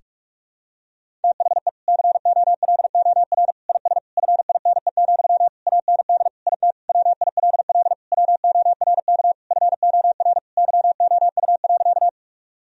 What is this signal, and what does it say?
Morse code, heard as a keyed tone, about 46 wpm, 700 Hz. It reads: THE COLOR IS FINE, AND IT WILL WORK FOR YOU.
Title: Sentences from Top 1000 Words - Review Part 2 - 45wpm